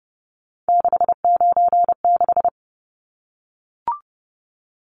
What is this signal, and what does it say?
Morse code, heard as a keyed tone, about 30 wpm, 700 Hz. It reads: 696
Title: Numbers - 3-Digit 30wpm